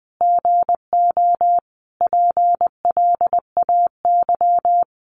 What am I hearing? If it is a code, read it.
Morse code, heard as a keyed tone, about 20 wpm, 700 Hz. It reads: GO PLAY
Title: Sentences from Top 1500 Words 20wpm